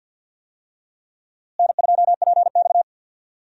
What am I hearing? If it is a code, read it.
Morse code, heard as a keyed tone, about 52 wpm, 700 Hz. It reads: N2PX
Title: USA Call Signs - 4 Characters - Repeated 3x - 50wpm